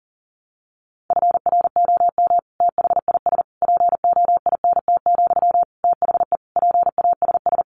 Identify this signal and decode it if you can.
Morse code, heard as a keyed tone, about 40 wpm, 700 Hz. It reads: FROM THIS POINT, THE PASS
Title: Sentences from Top 1200 Words 40wpm